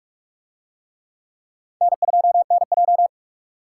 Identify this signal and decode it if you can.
Morse code, heard as a keyed tone, about 44 wpm, 700 Hz. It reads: N2NJ